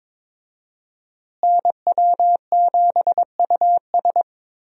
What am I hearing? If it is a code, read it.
Morse code, heard as a keyed tone, about 22 wpm, 700 Hz. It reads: NW7US
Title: USA Call Signs - 5 Characters - Repeated 3x - 22wpm